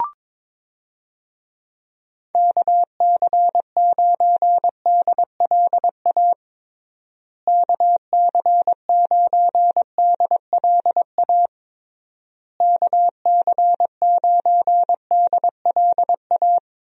Morse code, heard as a keyed tone, about 22 wpm, 700 Hz. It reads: KC9DLA KC9DLA KC9DLA
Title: USA Call Signs - 6 Characters - Repeated 3x - 22wpm